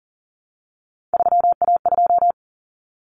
Morse code, heard as a keyed tone, about 40 wpm, 700 Hz. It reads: 3A2
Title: Common Cut Numbers - 3-Digit 40wpm